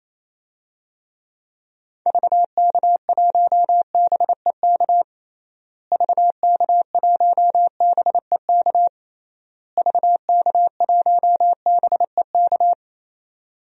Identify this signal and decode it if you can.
Morse code, heard as a keyed tone, about 28 wpm, 700 Hz. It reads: VK1BEK VK1BEK VK1BEK
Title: DXCC-100 Call Signs - 6 Characters - Repeated 3x - 28wpm